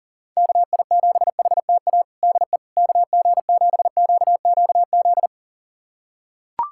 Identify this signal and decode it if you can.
Morse code, heard as a keyed tone, about 40 wpm, 700 Hz. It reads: KI7HTA DE KG7QQZ
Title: USA Callsign DE Callsign - 6 Characters - 40wpm